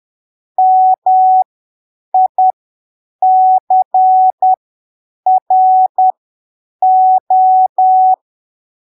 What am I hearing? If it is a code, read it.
Morse code, heard as a keyed tone, about 10 wpm, spaced as for 7 wpm, 750 Hz. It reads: MICRO
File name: arquivos / texto / texto 09 10ppm